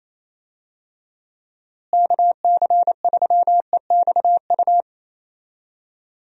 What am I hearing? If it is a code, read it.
Morse code, heard as a keyed tone, about 28 wpm, 700 Hz. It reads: KC3EXU
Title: USA Call Signs - 6 Characters - Repeated 2x - 28wpm